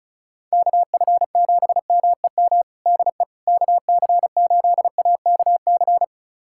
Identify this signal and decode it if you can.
Morse code, heard as a keyed tone, about 36 wpm, 700 Hz. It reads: KF7MEM DE KC8AKC
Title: USA Callsign DE Callsign - 6 Characters - 35wpm